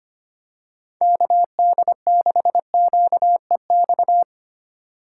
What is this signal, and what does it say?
Morse code, heard as a keyed tone, about 25 wpm, 700 Hz. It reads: KD6QEX
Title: USA Call Signs - 6 Characters - 3x Character Spacing - 25wpm